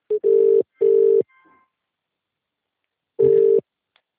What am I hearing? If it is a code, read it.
Ringback tone (UK), 400 + 450 Hz.